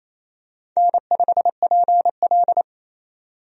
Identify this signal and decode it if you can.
Morse code, heard as a keyed tone, about 28 wpm, 700 Hz. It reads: N5PL
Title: USA Call Signs - 4 Characters - 28wpm